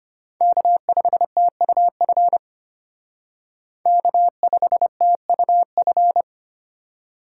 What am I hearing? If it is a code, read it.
Morse code, heard as a keyed tone, about 32 wpm, 700 Hz. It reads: K5TUF K5TUF